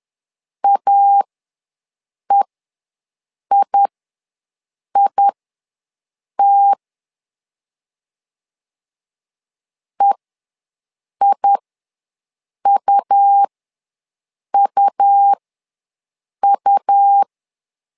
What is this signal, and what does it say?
Morse code, heard as a keyed tone, about 11 wpm, spaced as for 6 wpm, 800 Hz. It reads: AEIIT EIUUU
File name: LEZIONI / 2 UHDN / 2 U ws 10wpm 2017